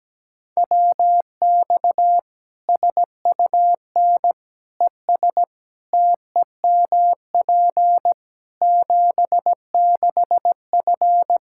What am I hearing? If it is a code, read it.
Morse code, heard as a keyed tone, about 17 wpm, 700 Hz. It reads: WX SUN ES TEMP 76F
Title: K1USN SST QSO Practice 17wpm